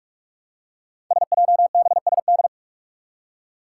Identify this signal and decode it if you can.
Morse code, heard as a keyed tone, about 44 wpm, 700 Hz. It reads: SJBSD
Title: Five Challenging Letter-Number Combinations 45wpm